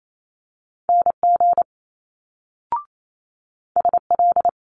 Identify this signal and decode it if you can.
Morse code, heard as a keyed tone, about 28 wpm, 700 Hz. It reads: NG SL
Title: Multi-letter Phonemes Quick Fire 28wpm